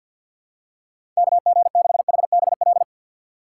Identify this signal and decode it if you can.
Morse code, heard as a keyed tone, about 53 wpm, 700 Hz. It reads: KG6HBL
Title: USA Call Signs - 6x Character Spacing - 50wpm